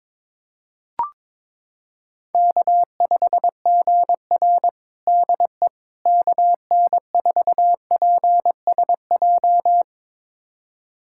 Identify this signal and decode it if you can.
Morse code, heard as a keyed tone, about 22 wpm, 700 Hz. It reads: K5GR DE KN4PSJ
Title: USA Callsign DE Callsign 22wpm